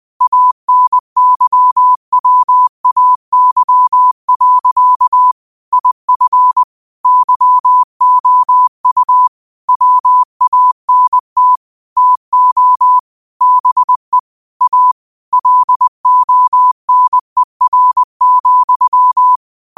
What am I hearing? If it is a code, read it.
Morse code, heard as a keyed tone, about 20 wpm, 1000 Hz. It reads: ANYWAY. IF YOU WANT TO BE A LONER,
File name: 20wpm